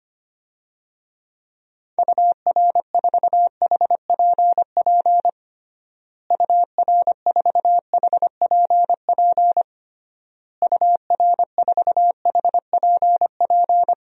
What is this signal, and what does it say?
Morse code, heard as a keyed tone, about 25 wpm, 700 Hz. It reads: UR4HPP UR4HPP UR4HPP
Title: DXCC-100 Call Signs - 6 Characters - Repeated 3x - 25wpm